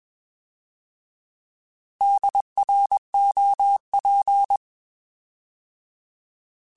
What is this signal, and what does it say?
Morse code, heard as a keyed tone, about 21 wpm, 800 Hz. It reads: DROP
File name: Alphabetical / Alphabetical mp3s / Ds @20